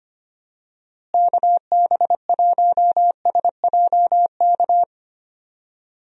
Morse code, heard as a keyed tone, about 25 wpm, 700 Hz. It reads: KB1SJK
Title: USA Call Signs - 6 Characters - Repeated 2x - 25wpm